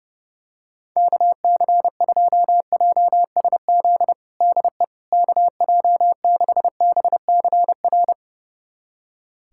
Morse code, heard as a keyed tone, about 30 wpm, 700 Hz. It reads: KC2JSZ DE KJ6BCR
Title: USA Callsign DE Callsign - 6 Characters - 30wpm